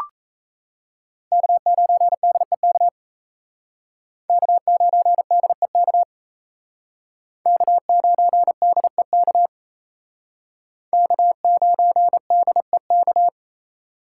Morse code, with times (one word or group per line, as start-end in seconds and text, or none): 1.3-2.9 K9DEK
4.3-6.0 K9DEK
7.5-9.5 K9DEK
10.9-13.3 K9DEK